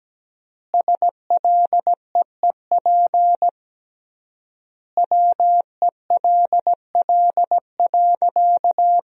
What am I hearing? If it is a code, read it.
Morse code, heard as a keyed tone, about 17 wpm, 700 Hz. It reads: SLEEP WELL.
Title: 3x Word-Spacing - Sentences from Top 500 Words - 17wpm